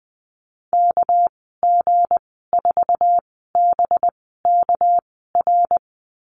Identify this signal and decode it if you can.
Morse code, heard as a keyed tone, about 20 wpm, spaced as for 14 wpm, 700 Hz. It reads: KG4BKR
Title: 238 Lesson 38g - Letter q - Call Signs 20wpm